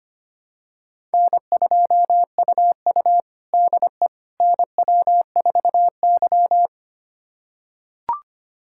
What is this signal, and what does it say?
Morse code, heard as a keyed tone, about 25 wpm, 700 Hz. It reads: N2UU DE NW4Y
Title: USA Callsign DE Callsign - 4 Characters - 25wpm